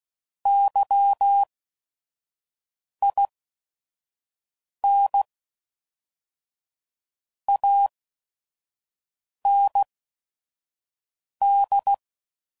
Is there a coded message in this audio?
Yes: Morse code from 0.5 to 11.9 s, heard as a keyed tone, about 16 wpm, spaced as for 5 wpm, 800 Hz. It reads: YIN AND